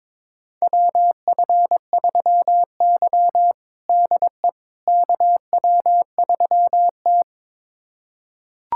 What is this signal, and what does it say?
Morse code, heard as a keyed tone, about 22 wpm, 700 Hz. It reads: WF3Y DE KW3T
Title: USA Callsign DE Callsign - 4 Characters - 22wpm